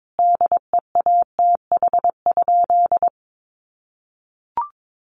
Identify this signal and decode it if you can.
Morse code, heard as a keyed tone, about 22 wpm, 700 Hz. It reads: DEATH?
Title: Sentences from Top 1100 Words 22wpm